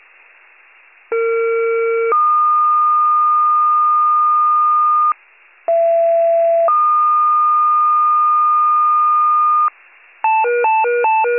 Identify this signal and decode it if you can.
Two-tone page, heard as a one-second tone, then a three-second tone, about 460 then 1220 Hz, about 670 then 1220 Hz. It is heard from 1.1 to 9.7 s.